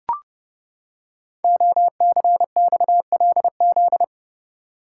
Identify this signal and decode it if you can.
Morse code, heard as a keyed tone, about 28 wpm, 700 Hz. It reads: OCXLZ
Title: Code Groups - 5 Letters - 30wpm